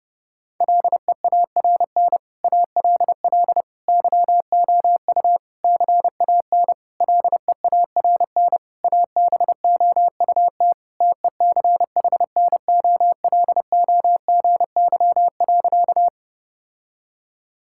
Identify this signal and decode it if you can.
Morse code, heard as a keyed tone, about 32 wpm, 700 Hz. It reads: LEARN ALL YOU CAN LEARN ABOUT TECHNOLOGY.
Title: Sentences from Top 1000 Words - Review Part 5 - 30wpm